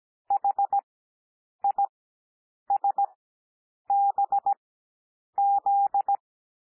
Morse code, heard as a keyed tone, about 17 wpm, spaced as for 8 wpm, 800 Hz. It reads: HISBZ
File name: cwmp3 / L2 SOUND LETTERS 17 wpm